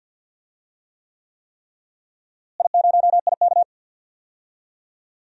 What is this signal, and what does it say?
Morse code, heard as a keyed tone, about 47 wpm, 700 Hz. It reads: I0IK
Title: DXCC-100 Call Signs - 4 Characters - Repeated 3x - 50wpm